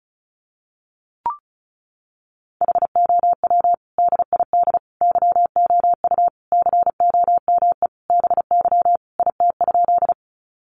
Morse code, heard as a keyed tone, about 34 wpm, 700 Hz. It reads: HOW DID YOU COME BY IT?
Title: Sentences from Top 100 Words 35wpm